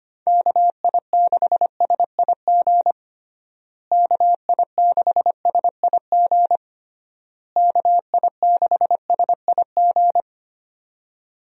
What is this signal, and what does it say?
Morse code, heard as a keyed tone, about 25 wpm, 700 Hz. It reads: KI6SIG KI6SIG KI6SIG
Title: USA Call Signs - 6 Characters - Repeated 3x - 25wpm